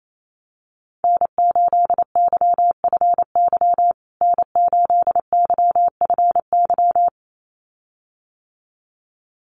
Morse code, heard as a keyed tone, about 28 wpm, 700 Hz. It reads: N8YFY N8YFY
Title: USA Call Signs Repeated 2x 28wpm